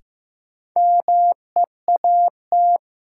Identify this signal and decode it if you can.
Morse code, heard as a keyed tone, about 15 wpm, 700 Hz. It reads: MEAT
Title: Sentences from Top 1200 Words - New Word Drill Alt - 15wpm